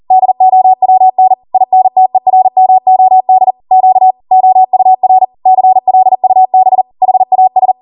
Morse code, heard as a keyed tone, about 40 wpm, 750 Hz. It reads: DOWN INTERMOD Q OUR CLUB HAS
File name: Morse / 40WPM